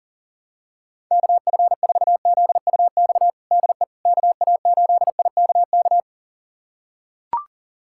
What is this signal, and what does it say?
Morse code, heard as a keyed tone, about 40 wpm, 700 Hz. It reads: KF4ZUX DE KA8IKK
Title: USA Callsign DE Callsign - 6 Characters - 40wpm